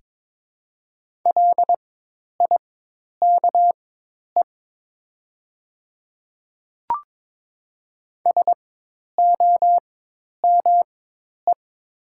Morse code, heard as a keyed tone, about 22 wpm, spaced as for 10 wpm, 700 Hz. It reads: LIKE SOME
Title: 4x Character Spacing - Top 100 Words - 22wpm